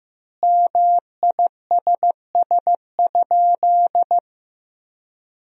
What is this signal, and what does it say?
Morse code, heard as a keyed tone, about 15 wpm, 700 Hz. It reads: MISS?
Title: 3x Word-Spacing - Sentences from Top 500 Words - 15wpm